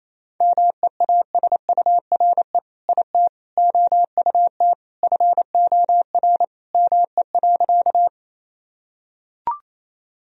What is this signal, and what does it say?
Morse code, heard as a keyed tone, about 27 wpm, 700 Hz. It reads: MEASURE IT OUT FOR ME.
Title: Sentences from Top 1000 Words - Review Part 2 - 28wpm